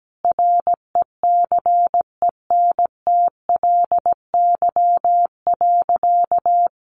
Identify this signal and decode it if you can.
Morse code, heard as a keyed tone, about 17 wpm, 700 Hz. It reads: RECENTLY.
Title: Sentences from Top 1200 Words 17wpm